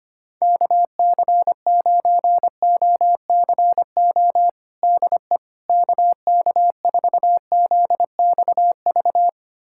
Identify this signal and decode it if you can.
Morse code, heard as a keyed tone, about 25 wpm, 700 Hz. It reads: KC9OCO DE KK4ZXV
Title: USA Callsign DE Callsign 25wpm